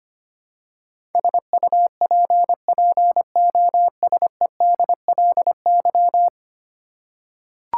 Morse code, heard as a keyed tone, about 25 wpm, 700 Hz. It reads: SUPPOSEDLY
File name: 10 Letter Words - First 3 Letters Given - 25wpm